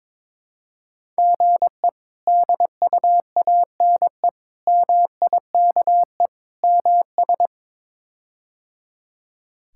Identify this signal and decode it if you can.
Morse code, heard as a keyed tone, about 22 wpm, 700 Hz. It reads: GE DUANE MIKE MS
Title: K1USN SST QSO Practice - 22wpm